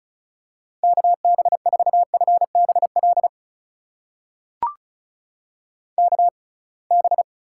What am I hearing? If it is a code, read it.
Morse code, heard as a keyed tone, about 36 wpm, spaced as for 27 wpm, 700 Hz. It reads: KB4FBL KB
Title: USA Call Signs - 6 Characters - 6x Character Spacing - 35wpm